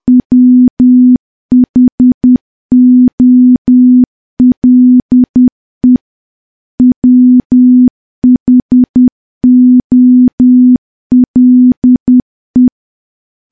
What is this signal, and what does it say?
Morse code, heard as a keyed tone, about 10 wpm, 250 Hz. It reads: WHOLE WHOLE
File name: morse / morse-training-10WPM